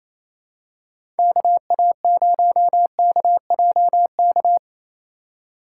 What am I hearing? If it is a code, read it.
Morse code, heard as a keyed tone, about 29 wpm, 700 Hz. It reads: KA0KJK
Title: USA Call Signs - 6 Characters - 3x Character Spacing - 28wpm